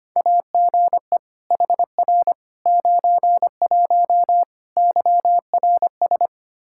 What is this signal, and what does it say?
Morse code, heard as a keyed tone, about 25 wpm, 700 Hz. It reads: AGE HR 91 YRS
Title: Practice QSO - Exchange 3 - More Info - 25wpm